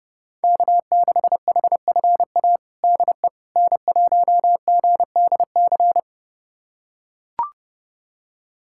Morse code, heard as a keyed tone, about 32 wpm, 700 Hz. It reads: K6HFA DE N1GDC
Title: USA Callsign DE Callsign - 5 Characters - 30wpm